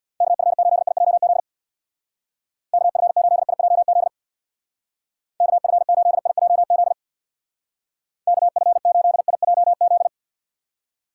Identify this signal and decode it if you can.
Morse code, heard as a keyed tone, about 62 wpm, 700 Hz. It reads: KF8IJZ KF8IJZ KF8IJZ KF8IJZ
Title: Speed Racing - USA 2x3 Call Signs 50wpm